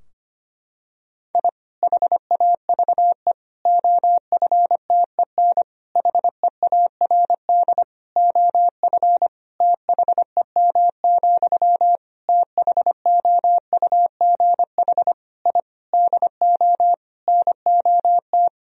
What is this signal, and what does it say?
Morse code, heard as a keyed tone, about 25 wpm, 700 Hz. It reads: I HAVE OFTEN HEARD OF THEM, THOUGH I DO NOT
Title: Sentences from Top 1000 Words - Review Part 2 - 25wpm